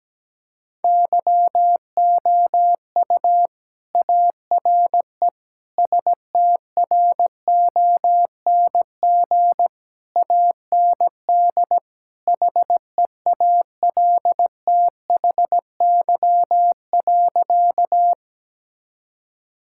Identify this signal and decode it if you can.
Morse code, heard as a keyed tone, about 17 wpm, 700 Hz. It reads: YOU ARE STRONG AND HEALTHY.
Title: Sentences from Top 1000 Words - Review Part 5 - 17wpm